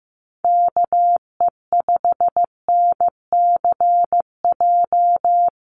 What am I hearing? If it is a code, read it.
Morse code, heard as a keyed tone, about 15 wpm, 700 Hz. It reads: KE5NCJ
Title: USA Call Signs Repeated 3x 15wpm